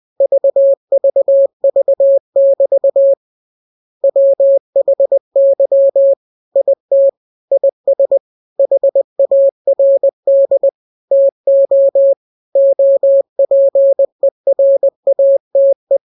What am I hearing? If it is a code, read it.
Morse code, heard as a keyed tone, about 20 wpm, 550 Hz. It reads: VVV<BT> WHY IT IS HARD TO OPERATE
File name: jokes / joke 5 20 0000